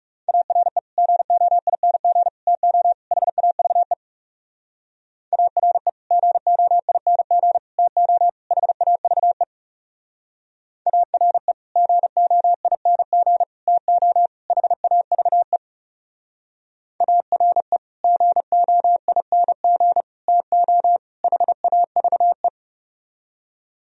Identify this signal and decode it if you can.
Morse code, heard as a keyed tone, about 40 wpm, 700 Hz. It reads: ARE GOING TO HAVE ARE GOING TO HAVE ARE GOING TO HAVE ARE GOING TO HAVE